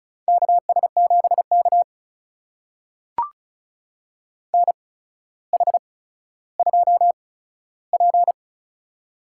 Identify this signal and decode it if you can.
Morse code, heard as a keyed tone, about 34 wpm, spaced as for 15 wpm, 700 Hz. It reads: KS7K NH2P
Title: USA Call Signs - 8x Character Spacing - 35wpm